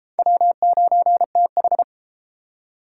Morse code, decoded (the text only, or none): W9TH